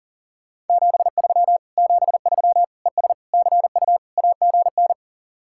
Morse code, heard as a keyed tone, about 49 wpm, 700 Hz. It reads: 73 73 ES CU AGN
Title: Speed-Racing Common QSO Phrases Encore - 40wpm